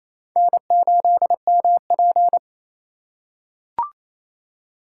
Morse code, heard as a keyed tone, about 28 wpm, 700 Hz. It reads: N8MP
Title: USA Call Signs - 4x Character Spacing - 28wpm